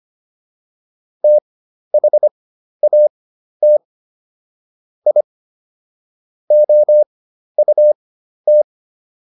Morse code, heard as a keyed tone, about 25 wpm, spaced as for 12 wpm, 600 Hz. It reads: THAT I OUT